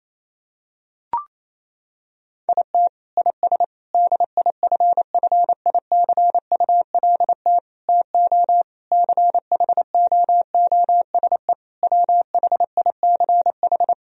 Morse code, heard as a keyed tone, about 28 wpm, 700 Hz. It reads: IT IS DIFFICULT TO CHOOSE WHICH
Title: Sentences from Top 1000 Words - Review Part 3 - 28wpm